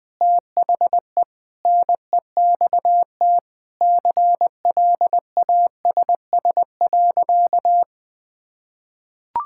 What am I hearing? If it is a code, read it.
Morse code, heard as a keyed tone, about 21 wpm, 700 Hz. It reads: THE NEXT CLASS.
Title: Sentences from Top 1000 Words - Review Part 6 - 20wpm